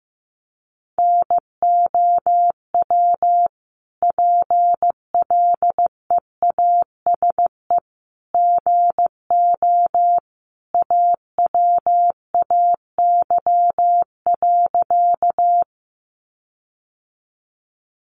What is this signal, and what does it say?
Morse code, heard as a keyed tone, about 15 wpm, 700 Hz. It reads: NOW PLEASE GO AWAY.
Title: Sentences from Top 1500 Words 15wpm